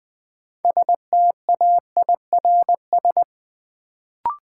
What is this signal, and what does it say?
Morse code, heard as a keyed tone, about 20 wpm, 700 Hz. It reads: STAIRS
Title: Sets of 5 Words Encore - New Word Drill - 20wpm